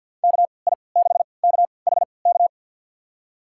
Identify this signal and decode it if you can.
Morse code, heard as a keyed tone, about 52 wpm, spaced as for 28 wpm, 690 Hz. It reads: KI6KHK
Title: USA Call Signs - 6 Characters - 3x Character Spacing - 50wpm